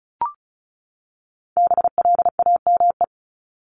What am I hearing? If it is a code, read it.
Morse code, heard as a keyed tone, about 34 wpm, 700 Hz. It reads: BLAME
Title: Top 1751-2000 Words 35wpm